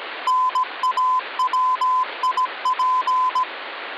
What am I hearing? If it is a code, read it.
Morse code (Russian alphabet), heard as a keyed tone, about 17 wpm, 1020 Hz, with noise. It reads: НАВИП
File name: original